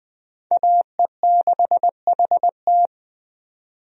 Morse code, heard as a keyed tone, about 20 wpm, 700 Hz. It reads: AE6HT
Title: USA Call Signs - 5 Characters - 20wpm